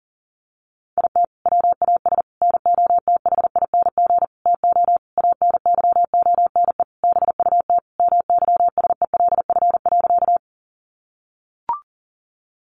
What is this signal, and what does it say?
Morse code, heard as a keyed tone, about 40 wpm, 700 Hz. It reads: IT WAS NOTHING TO ANYONE BUT MYSELF.
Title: Sentences from Top 1300 Words 40wpm